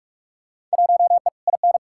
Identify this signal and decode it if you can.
Morse code, heard as a keyed tone, about 45 wpm, 700 Hz. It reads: 1E IN
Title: Field Day Exchanges - 45wpm